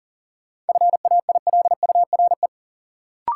Morse code, heard as a keyed tone, about 40 wpm, 700 Hz. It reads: FAILURE
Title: Sentences from Top 900 Words - New Word Drill - 40wpm